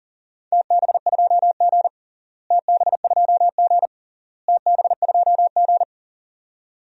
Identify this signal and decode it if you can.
Morse code, heard as a keyed tone, about 40 wpm, 700 Hz. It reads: TB2G TB2G TB2G
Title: DXCC-100 Call Signs - 4 Characters - Repeated 3x - 40wpm